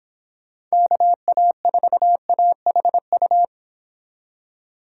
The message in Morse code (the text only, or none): KA4AHU